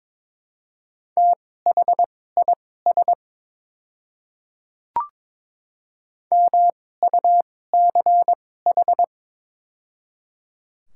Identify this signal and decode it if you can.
Morse code, heard as a keyed tone, about 22 wpm, spaced as for 16 wpm, 700 Hz. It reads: THIS MUCH
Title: 2x Character Spacing - Top 300 Words - 22wpm